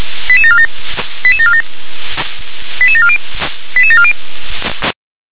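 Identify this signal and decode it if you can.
Five-tone selective call (ZVEI), heard as quick runs of five tones, tones of about 70 ms. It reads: 88747 88747 88740 88740
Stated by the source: ZVEI five-tone selective call received on an SDR.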